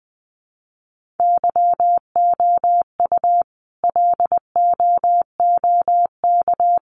Morse code, heard as a keyed tone, about 19 wpm, 700 Hz. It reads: YOU LOOK
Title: Sentences from Top 1500 Words 20wpm